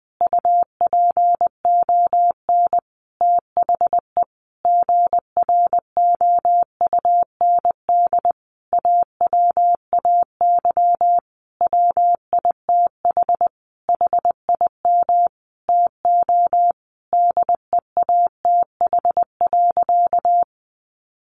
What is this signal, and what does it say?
Morse code, heard as a keyed tone, about 19 wpm, 700 Hz. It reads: UPON THE GROUND AWAY WITH HIM TO DEATH.
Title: Sentences from Top 1100 Words 20wpm